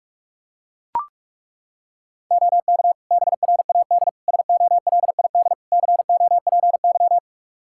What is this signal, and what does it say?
Morse code, heard as a keyed tone, about 41 wpm, 700 Hz. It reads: OK BRAD SOLID COPY
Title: Practice QSO - Exchange 3 - More Info - 45wpm